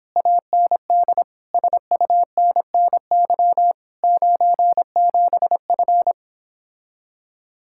Morse code, heard as a keyed tone, about 26 wpm, 700 Hz. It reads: AND SUNNY 97F
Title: Speed-Racing Common QSO Phrases Encore - 22wpm